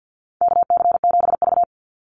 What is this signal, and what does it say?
Morse code, heard as a keyed tone, about 50 wpm, 700 Hz. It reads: KC7V
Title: Call Signs - 50wpm